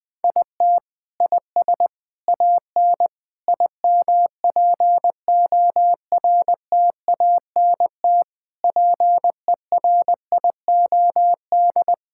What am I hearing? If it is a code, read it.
Morse code, heard as a keyed tone, about 20 wpm, 700 Hz. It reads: IT IS AN IMPORTANT PERIOD
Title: Sentences from Top 1000 Words - Review Part 5 - 20wpm